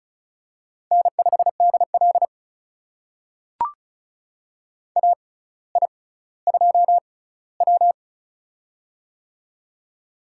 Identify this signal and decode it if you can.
Morse code, heard as a keyed tone, about 36 wpm, spaced as for 18 wpm, 700 Hz. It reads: N5DL AI2W
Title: USA Call Signs - 4 Characters - 6x Character Spacing - 35wpm